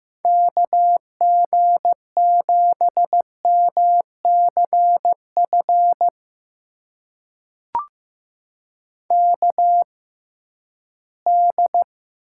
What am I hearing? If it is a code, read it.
Morse code, heard as a keyed tone, about 15 wpm, spaced as for 11 wpm, 700 Hz. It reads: KG7MCF KD
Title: USA Call Signs - 6x Character Spacing - 15wpm